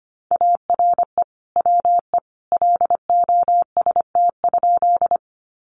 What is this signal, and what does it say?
Morse code, heard as a keyed tone, about 25 wpm, 700 Hz. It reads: ARE WE LOST?
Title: Sentences from Top 1400 Words 25wpm